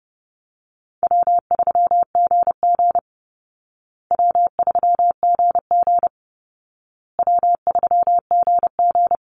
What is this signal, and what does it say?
Morse code, heard as a keyed tone, about 30 wpm, 700 Hz. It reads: W3GG W3GG W3GG
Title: Easy Call Signs Repeated 3x - 30wpm